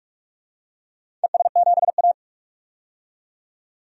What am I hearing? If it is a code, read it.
Morse code, heard as a keyed tone, about 44 wpm, 690 Hz. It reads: ES7A
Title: DXCC-100 Call Signs - 4 Characters - 45wpm